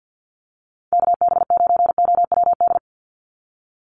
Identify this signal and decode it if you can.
Morse code, heard as a keyed tone, about 50 wpm, 700 Hz. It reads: KB9OWD
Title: Call Signs - 50wpm